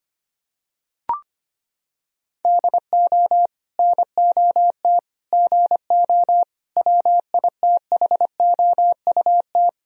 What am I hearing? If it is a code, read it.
Morse code, heard as a keyed tone, about 25 wpm, 700 Hz. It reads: DO NOT GO WITHOUT
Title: Sentences from Top 1000 Words - Review Part 3 - 25wpm